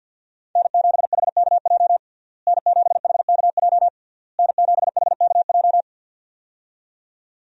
Morse code, heard as a keyed tone, about 52 wpm, 700 Hz. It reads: N7HKJ N7HKJ N7HKJ
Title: USA Call Signs - 5 Characters - Repeated 3x - 50wpm